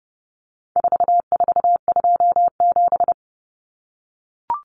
Morse code, heard as a keyed tone, about 30 wpm, 700 Hz. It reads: 4427
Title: Numbers - 4-Digit 30wpm